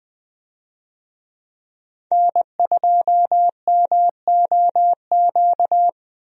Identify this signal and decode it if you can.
Morse code, heard as a keyed tone, about 20 wpm, 700 Hz. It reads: N2MOQ
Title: USA Call Signs - 5 Characters - 20wpm